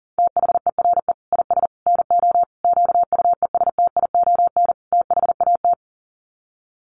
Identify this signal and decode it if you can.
Morse code, heard as a keyed tone, about 40 wpm, 700 Hz. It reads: THERE IS NO QUESTION THAT
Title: Sets of 5 Words Encore 40wpm